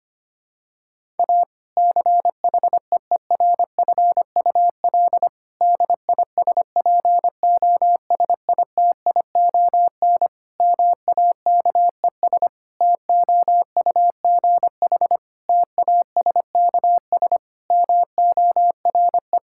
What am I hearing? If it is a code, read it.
Morse code, heard as a keyed tone, about 25 wpm, 700 Hz. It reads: A CHEERFUL DISPOSITION MAKES TOUGH TASKS MORE
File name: practice / 25wpm / Difficult Sentences with Hints - 25wpm